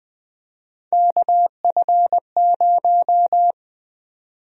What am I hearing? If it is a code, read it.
Morse code, heard as a keyed tone, about 20 wpm, 700 Hz. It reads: KF0
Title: USA Call Sign Prefixes - 20wpm